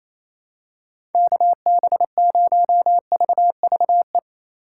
Morse code, heard as a keyed tone, about 27 wpm, 700 Hz. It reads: KB0VVE
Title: USA Call Signs - 6 Characters - 4x Character Spacing - 28wpm